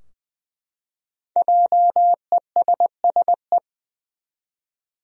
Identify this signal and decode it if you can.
Morse code, heard as a keyed tone, about 20 wpm, 700 Hz. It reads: JESSE
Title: USA Common Names - 20wpm